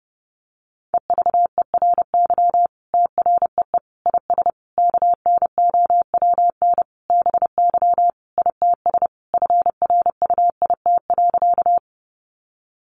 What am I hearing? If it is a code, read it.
Morse code, heard as a keyed tone, about 28 wpm, 700 Hz. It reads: EVERY TREE IS KNOWN BY ITS FRUIT.
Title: Sentences from Top 1300 Words 30wpm